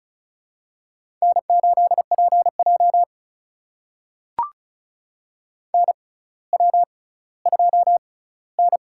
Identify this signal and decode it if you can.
Morse code, heard as a keyed tone, about 36 wpm, spaced as for 18 wpm, 700 Hz. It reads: N8PJ NW2N
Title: USA Call Signs - 4 Characters - 6x Character Spacing - 35wpm